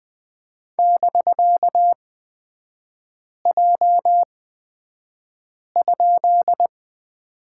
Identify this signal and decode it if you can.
Morse code, heard as a keyed tone, about 20 wpm, spaced as for 5 wpm, 700 Hz. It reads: <BK>J?
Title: 255 Lesson 41e - Period - Three Characters 20wpm